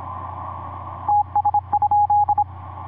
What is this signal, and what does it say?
Morse code, heard as a keyed tone, about 24 wpm, 830 Hz. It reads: TS?
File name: vu2pai ik5pwq etc 17m cw 16jan2010 1003